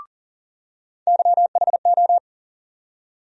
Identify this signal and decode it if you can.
Morse code, heard as a keyed tone, about 40 wpm, 700 Hz. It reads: YHO
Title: USA Call Sign Postfixes - 3 Letters 40wpm